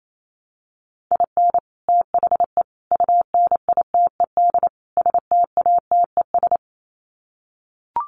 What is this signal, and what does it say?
Morse code, heard as a keyed tone, about 28 wpm, 700 Hz. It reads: IN THE UNITED STATES
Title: Sets of 4 Words 28wpm